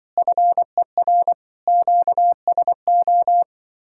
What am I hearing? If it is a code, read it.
Morse code, heard as a keyed tone, about 24 wpm, 700 Hz. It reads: FER QSO
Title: Speed-Racing Common QSO Phrases Encore - 20wpm